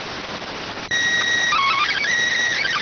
SSTV (slow-scan television), PD 290.